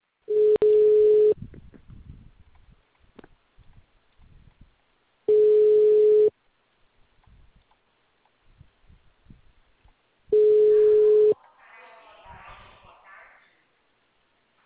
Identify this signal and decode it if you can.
Ringback tone (Europe, 425 Hz), with speech.